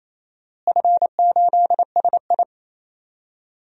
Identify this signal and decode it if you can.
Morse code, heard as a keyed tone, about 28 wpm, 700 Hz. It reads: F8SI
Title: DXCC-100 Call Signs - 4 Characters - Repeated 3x - 28wpm